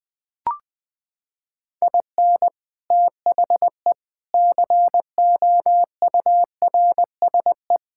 Morse code, heard as a keyed tone, about 20 wpm, 700 Hz. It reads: IN THE COURSE
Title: Sentences with 3-Word Phrases - 20wpm